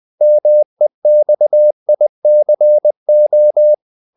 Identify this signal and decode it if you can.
Morse code, heard as a keyed tone, about 20 wpm, 600 Hz. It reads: MEXICO
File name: CW / Practice / NationalCapitals20